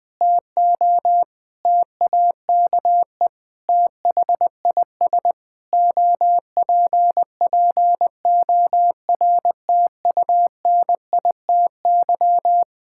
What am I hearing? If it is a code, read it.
Morse code, heard as a keyed tone, about 20 wpm, 700 Hz. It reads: TO TAKE THIS OPPORTUNITY
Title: Sets of 8 Words - 20wpm